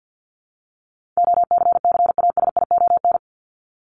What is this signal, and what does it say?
Morse code, heard as a keyed tone, about 50 wpm, 700 Hz. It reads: OCCASION
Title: Top 1501-1750 Words 50wpm